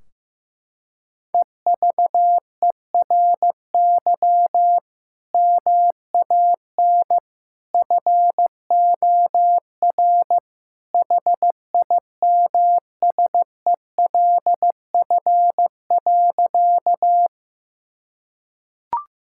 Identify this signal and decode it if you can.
Morse code, heard as a keyed tone, about 15 wpm, 700 Hz. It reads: EVERY MAN FOR HIMSELF.
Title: Sentences from Top 1600 Words 15wpm